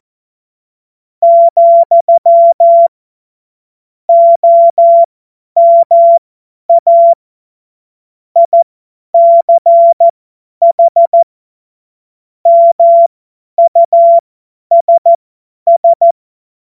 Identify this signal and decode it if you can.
Morse code, heard as a keyed tone, about 14 wpm, spaced as for 10 wpm, 680 Hz. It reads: , OMA ICH MUSS
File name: Morsekurs / Witze / Witz 12. Oktober 2021, 50 F70